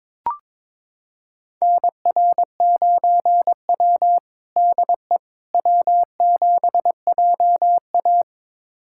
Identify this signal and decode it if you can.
Morse code, heard as a keyed tone, about 22 wpm, 700 Hz. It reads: NR9W DE W7JA
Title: USA Callsign DE Callsign - 4 Characters - 22wpm